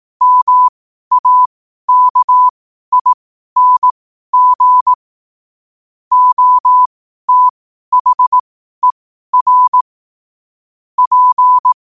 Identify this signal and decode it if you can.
Morse code, heard as a keyed tone, about 18 wpm, spaced as for 12 wpm, 1000 Hz. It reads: MAKING OTHER P